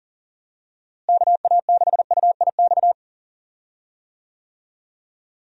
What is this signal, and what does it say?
Morse code, heard as a keyed tone, about 40 wpm, 700 Hz. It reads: KA6UIX
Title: USA Call Signs - 6 Characters - Repeated 2x - 40wpm